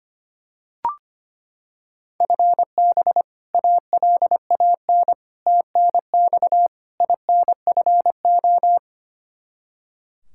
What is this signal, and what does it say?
Morse code, heard as a keyed tone, about 25 wpm, 700 Hz. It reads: FB ALAN TNX INFO
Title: Practice QSO - Exchange 3 - More Info - 25wpm